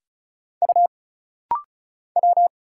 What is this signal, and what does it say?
Morse code, heard as a keyed tone, about 36 wpm, 700 Hz. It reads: U W
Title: Single Letter-Number Alt - Rapid-Fire 35wpm